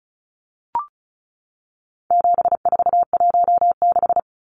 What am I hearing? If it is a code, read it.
Morse code, heard as a keyed tone, about 34 wpm, 700 Hz. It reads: 7416
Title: Numbers - 4-Digit 35wpm